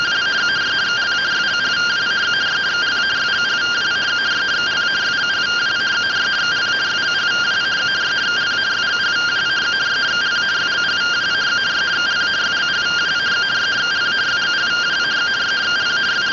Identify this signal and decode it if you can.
Two-tone teleprinter signal, about 170 Hz shift.